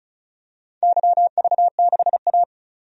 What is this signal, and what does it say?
Morse code, heard as a keyed tone, about 34 wpm, 700 Hz. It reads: YV6A
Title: DXCC-100 Call Signs - 4 Characters - 35wpm